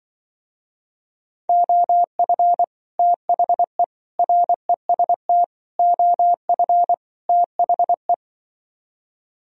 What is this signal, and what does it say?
Morse code, heard as a keyed tone, about 24 wpm, 700 Hz. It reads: OF THE REST OF THE
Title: Speed-Racing - Sets of 5 Words 20wpm